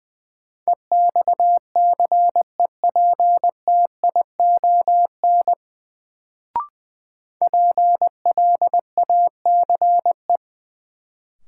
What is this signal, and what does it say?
Morse code, heard as a keyed tone, about 20 wpm, 700 Hz. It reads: EXCEPTION PLACE
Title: Sets of 4 Words - New Word Drill - 20wpm